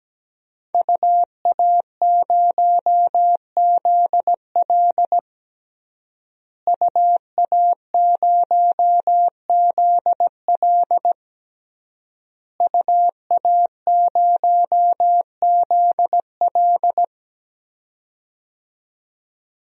Morse code, heard as a keyed tone, about 17 wpm, 700 Hz. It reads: UA0ZL UA0ZL UA0ZL
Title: DXCC-100 Call Signs - 5 Characters - Repeated 3x - 17wpm